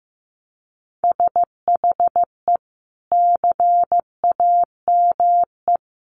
Morse code, heard as a keyed tone, about 15 wpm, 700 Hz. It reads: SHE CAME
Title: Sentences from Top 300 Words 15wpm